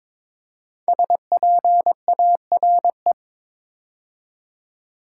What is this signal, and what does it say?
Morse code, heard as a keyed tone, about 22 wpm, 700 Hz. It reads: SPARE
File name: 5 Letter Words - First 3 Letters Given - 22wpm